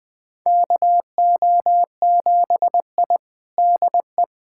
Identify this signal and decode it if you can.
Morse code, heard as a keyed tone, about 20 wpm, 700 Hz. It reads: KO7I DE
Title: USA Callsign DE Callsign - 4 Characters - 20wpm